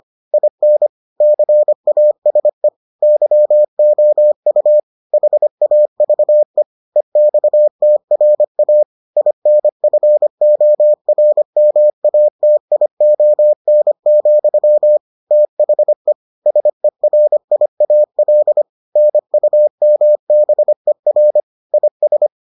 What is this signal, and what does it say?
Morse code, heard as a keyed tone, about 25 wpm, 600 Hz. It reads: IN CASE YOU HAVE EXTRA INFORMATION, THE SERIAL NUMBER IS